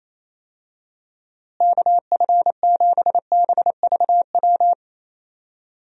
Morse code, heard as a keyed tone, about 28 wpm, 700 Hz. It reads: KF7BVW
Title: USA Call Signs - 6 Characters - Repeated 3x - 28wpm